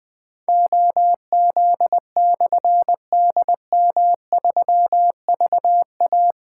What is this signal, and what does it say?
Morse code, heard as a keyed tone, about 20 wpm, 700 Hz. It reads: OZ/DM3VA
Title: Difficult Call Signs 20wpm